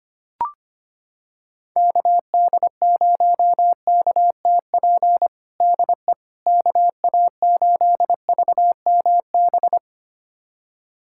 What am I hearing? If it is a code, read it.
Morse code, heard as a keyed tone, about 25 wpm, 700 Hz. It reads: KD0KTP DE KA8VMB
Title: USA Callsign DE Callsign 25wpm